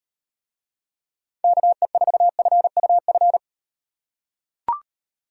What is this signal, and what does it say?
Morse code, heard as a keyed tone, about 38 wpm, 700 Hz. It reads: KE4FUF